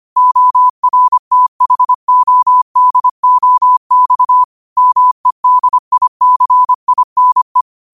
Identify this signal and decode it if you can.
Morse code, heard as a keyed tone, about 25 wpm, 1000 Hz. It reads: ORTHODOX MEDICINE